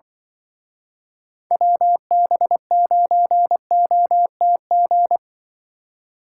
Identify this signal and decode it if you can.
Morse code, heard as a keyed tone, about 24 wpm, 700 Hz. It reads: WB9OTG
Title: Speed Racing - USA 2x3 Call Signs 20wpm